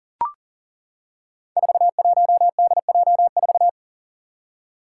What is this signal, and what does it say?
Morse code, heard as a keyed tone, about 37 wpm, 700 Hz. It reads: 41DJ4
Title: Five Challenging Letter-Number Combinations 40wpm